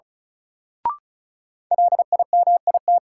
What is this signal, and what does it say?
Morse code, heard as a keyed tone, about 34 wpm, 700 Hz. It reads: LIMIT
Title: Sentences from Top 1500 Words - New Word Drill - 35wpm